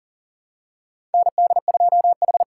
Morse code, heard as a keyed tone, about 40 wpm, 700 Hz. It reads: ND2H
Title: USA Call Signs - 4 Characters - 4x Character Spacing - 40wpm